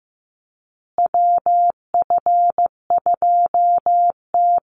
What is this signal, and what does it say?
Morse code, heard as a keyed tone, about 15 wpm, 700 Hz. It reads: WF2T
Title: USA Call Signs Repeated 2x 15wpm